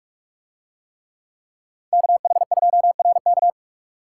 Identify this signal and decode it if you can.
Morse code, heard as a keyed tone, about 42 wpm, 700 Hz. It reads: KH2RK